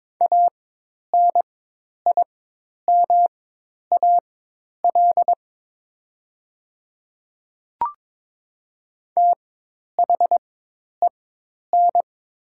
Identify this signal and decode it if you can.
Morse code, heard as a keyed tone, about 22 wpm, spaced as for 10 wpm, 700 Hz. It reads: ANIMAL THEN
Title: 4x Character Spacing - Top 200 Words - 22wpm